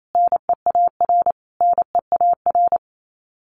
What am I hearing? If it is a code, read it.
Morse code, heard as a keyed tone, about 28 wpm, 700 Hz. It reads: NEAR NEAR
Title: Top 200 Words Repeated 28wpm